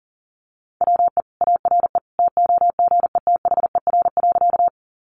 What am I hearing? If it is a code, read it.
Morse code, heard as a keyed tone, about 40 wpm, 700 Hz. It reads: WE ARE TOGETHER.
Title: Sentences from Top 300 Words 40wpm